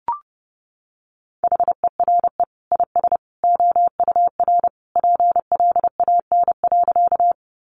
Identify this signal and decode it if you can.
Morse code, heard as a keyed tone, about 30 wpm, 700 Hz. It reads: HERE IS OUR PLAN.
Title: Sentences from Top 500 Words 30wpm